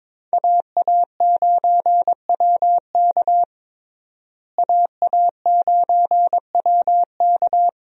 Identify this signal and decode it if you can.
Morse code, heard as a keyed tone, about 22 wpm, 700 Hz. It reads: AA9WK AA9WK
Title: USA Call Signs - 5 Characters - Repeated 3x - 22wpm